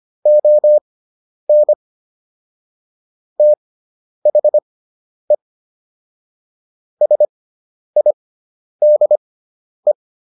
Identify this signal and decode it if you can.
Morse code, heard as a keyed tone, about 25 wpm, spaced as for 10 wpm, 600 Hz. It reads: ON THE SIDE